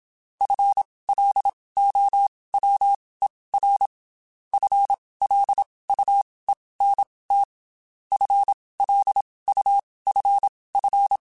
Morse code, heard as a keyed tone, about 27 wpm, spaced as for 19 wpm, 800 Hz. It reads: FLOWER FLUENT FLUFF
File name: Spaced / FL Runs sp mp3s / FL Runs sp @25